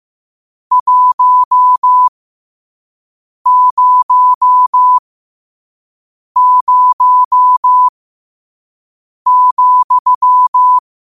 Morse code, heard as a keyed tone, about 15 wpm, spaced as for 5 wpm, 1000 Hz. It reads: 100,
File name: qotdfiles / 5wpm